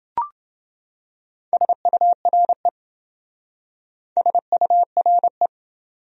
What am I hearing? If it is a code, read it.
Morse code, heard as a keyed tone, about 28 wpm, 700 Hz. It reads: SURE SURE